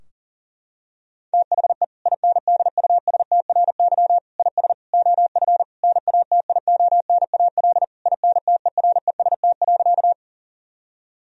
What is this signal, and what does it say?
Morse code, heard as a keyed tone, about 40 wpm, 700 Hz. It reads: THE INDUSTRY IS OF NATIONAL INTEREST.
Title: Sentences from Top 1000 Words - Review Part 4 - 40wpm